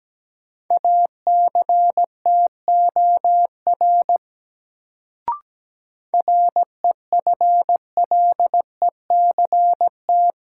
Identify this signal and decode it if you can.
Morse code, heard as a keyed tone, about 17 wpm, 700 Hz. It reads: ACTOR REFLECT
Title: Sentences from Top 1500 Words - New Word Drill - 17wpm